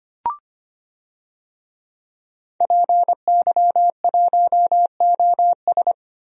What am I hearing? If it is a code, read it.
Morse code, heard as a keyed tone, about 25 wpm, 700 Hz. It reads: PY1OS
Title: DXCC-100 Call Signs - 5 Characters - Repeated 3x - 25wpm